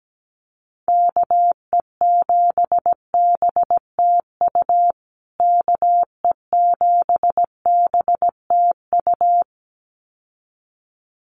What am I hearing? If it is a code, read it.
Morse code, heard as a keyed tone, about 17 wpm, 700 Hz. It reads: KE7BTU KE7BTU
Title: USA Call Signs Repeated 2x 17wpm